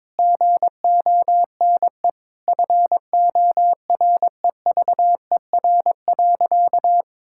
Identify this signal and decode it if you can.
Morse code, heard as a keyed tone, about 22 wpm, 700 Hz. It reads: GONE FOREVER.
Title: Sentences from Top 1600 Words 22wpm